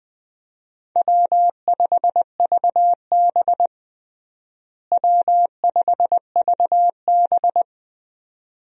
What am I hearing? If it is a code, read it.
Morse code, heard as a keyed tone, about 21 wpm, 700 Hz. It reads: W5VB W5VB
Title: USA Call Signs - 4 Characters - Repeated 2x - 20wpm